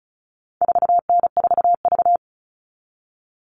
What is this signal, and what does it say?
Morse code, heard as a keyed tone, about 34 wpm, 700 Hz. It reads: 4N4V
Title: All Cut Numbers - 4-Digit 35wpm